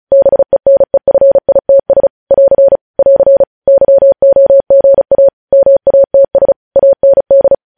Morse code, heard as a keyed tone, about 36 wpm, 550 Hz. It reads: BENEFITS <AR> <AR> YOGA MATS AND
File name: podcast / News headlines-1x-35x35